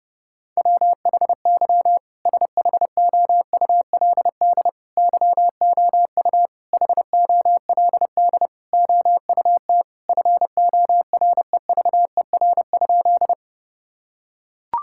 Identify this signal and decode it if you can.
Morse code, heard as a keyed tone, about 30 wpm, 700 Hz. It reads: WHY SHOULD YOU HOLD OUT FOREVER?
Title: Sentences from Top 1600 Words 30wpm